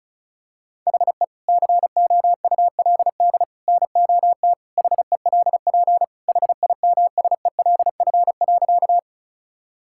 Morse code, heard as a keyed tone, about 36 wpm, 700 Hz. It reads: HE COULD NOT HELP HIMSELF.
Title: Sentences from Top 1600 Words 35wpm